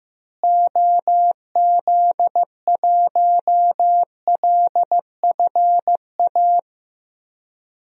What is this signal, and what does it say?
Morse code, heard as a keyed tone, about 15 wpm, 700 Hz. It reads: OZ1LFA
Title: DXCC-100 Call Signs - 6 Characters - Repeated 3x - 15wpm